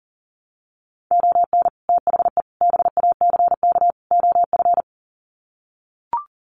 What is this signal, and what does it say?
Morse code, heard as a keyed tone, about 40 wpm, 700 Hz. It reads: ON THE BACK OF
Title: Sets of 4 Words 40wpm